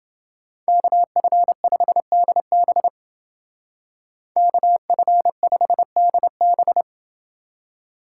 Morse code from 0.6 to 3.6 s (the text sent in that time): KF5DB